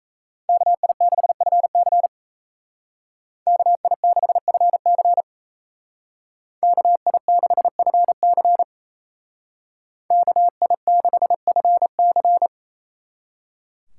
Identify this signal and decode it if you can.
Morse code, heard as a keyed tone, about 35 wpm, 700 Hz. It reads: KI6FC KI6FC KI6FC KI6FC